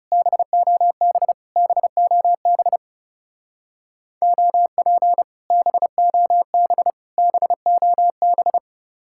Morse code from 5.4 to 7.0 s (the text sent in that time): BOB